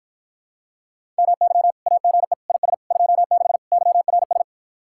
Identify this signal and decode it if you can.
Morse code, heard as a keyed tone, about 53 wpm, 700 Hz. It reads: MY AGE IS 26 YRS